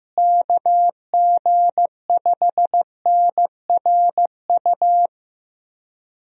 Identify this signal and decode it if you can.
Morse code, heard as a keyed tone, about 14 wpm, 700 Hz. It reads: KG5NRU
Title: USA Call Signs - 6 Characters - Repeated 3x - 15wpm